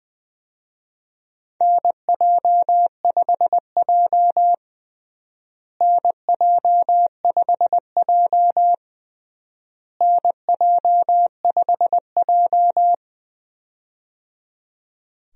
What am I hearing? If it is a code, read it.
Morse code, heard as a keyed tone, about 21 wpm, 700 Hz. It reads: NJ5J NJ5J NJ5J
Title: USA Call Signs - 4 Characters - Repeated 3x - 20wpm